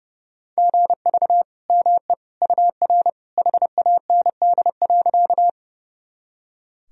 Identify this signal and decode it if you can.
Morse code, heard as a keyed tone, about 30 wpm, 700 Hz. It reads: GV ME UR HAND.
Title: Sentences from Top 200 Words and Abbreviations 30wpm